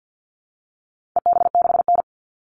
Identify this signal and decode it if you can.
Morse code, heard as a keyed tone, about 50 wpm, 700 Hz. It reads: EB6N
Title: All Cut Numbers - 4-Digit 50wpm